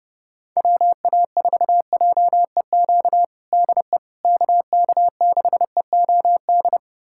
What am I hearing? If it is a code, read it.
Morse code, heard as a keyed tone, about 30 wpm, 700 Hz. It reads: WA4JEQ DE KK6EOD
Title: USA Callsign DE Callsign 30wpm